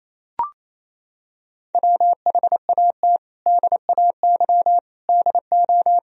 Morse code, heard as a keyed tone, about 28 wpm, 700 Hz. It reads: WHAT DAY DO
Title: Sentences from Top 1000 Words - Review Part 3 - 28wpm